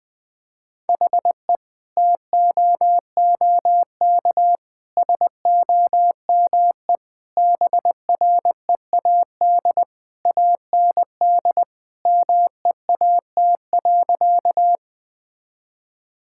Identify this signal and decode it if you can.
Morse code, heard as a keyed tone, about 20 wpm, 700 Hz. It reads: HE TOOK SOME BREAD AND MEAT.
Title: Sentences from Top 1600 Words 20wpm